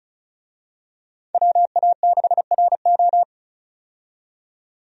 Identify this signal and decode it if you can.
Morse code, heard as a keyed tone, about 34 wpm, 700 Hz. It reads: WA6RO
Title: USA Call Signs - 5 Characters - 35wpm